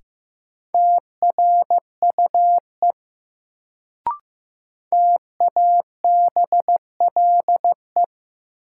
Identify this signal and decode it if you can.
Morse code, heard as a keyed tone, about 15 wpm, 700 Hz. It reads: TRUE TABLE
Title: Sentences from Top 400 Words - New Word Drill - 15wpm